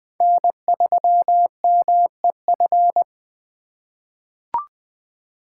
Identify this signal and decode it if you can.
Morse code, heard as a keyed tone, about 21 wpm, 700 Hz. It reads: N3MEF
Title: USA Call Signs - 8x Character Spacing - 20wpm